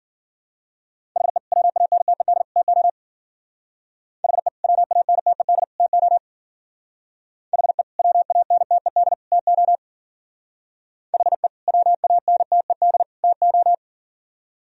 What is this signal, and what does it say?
Morse code, heard as a keyed tone, about 51 wpm, 700 Hz. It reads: HE WANTED TO HE WANTED TO HE WANTED TO HE WANTED TO